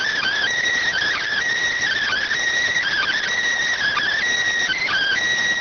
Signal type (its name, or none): Emergency Alert System (SAME) tones